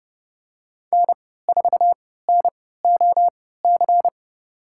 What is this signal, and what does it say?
Morse code, heard as a keyed tone, about 30 wpm, spaced as for 17 wpm, 700 Hz. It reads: N4NOC
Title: USA Call Signs - 5 Characters - 3x Character Spacing - 30wpm